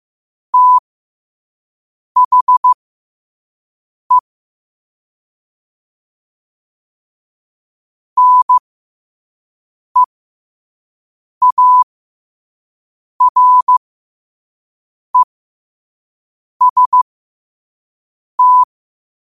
Morse code, heard as a keyed tone, about 15 wpm, spaced as for 5 wpm, 1000 Hz. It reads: THE NEAREST
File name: qotdfiles / 5wpm